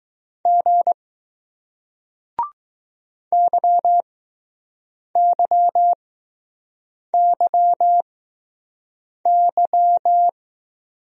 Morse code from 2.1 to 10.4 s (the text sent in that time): YYYY